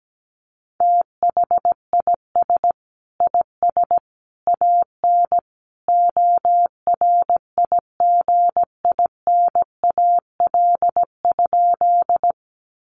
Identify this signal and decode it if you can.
Morse code, heard as a keyed tone, about 17 wpm, 700 Hz. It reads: THIS IS AN ORIGINAL?
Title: Sentences from Top 900 Words 17wpm